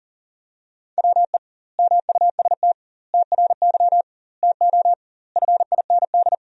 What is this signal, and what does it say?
Morse code, heard as a keyed tone, about 40 wpm, 700 Hz. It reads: WE MUST TRY TO FIND
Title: 2x Word-Spacing - Sentences from Top 400 Words - 40wpm